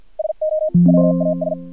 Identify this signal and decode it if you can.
Morse code, heard as a keyed tone, about 55 wpm, spaced as for 47 wpm, 620 Hz. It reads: SO WAS
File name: archived / CRSnet